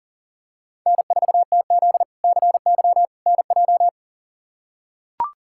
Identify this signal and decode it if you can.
Morse code, heard as a keyed tone, about 40 wpm, 700 Hz. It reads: N4TZ CY NJ
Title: K1USN SST QSO Practice - 40wpm